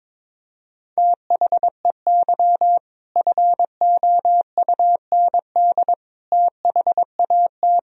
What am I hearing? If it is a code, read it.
Morse code, heard as a keyed tone, about 22 wpm, 700 Hz. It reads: THEY FOUND THAT
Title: Sentences from Top 1000 Words - Review Part 4 - 22wpm